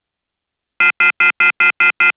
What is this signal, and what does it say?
Off-hook warning tone, 1400 + 2060 + 2450 + 2600 Hz, with noise.